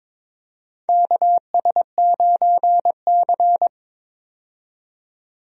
Morse code, heard as a keyed tone, about 22 wpm, 700 Hz. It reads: KS9C
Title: USA Call Signs - 4 Characters - 22wpm